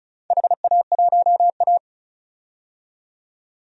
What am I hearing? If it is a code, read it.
Morse code, heard as a keyed tone, about 36 wpm, 690 Hz. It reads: HA1A
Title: DXCC-100 Call Signs - 4 Characters - 35wpm